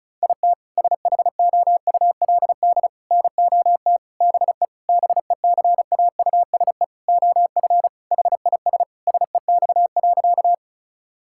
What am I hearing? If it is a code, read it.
Morse code, heard as a keyed tone, about 34 wpm, 700 Hz. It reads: IT SHOULD NOT BE BECAUSE OF HIS SEX.
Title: Sentences from Top 1600 Words 35wpm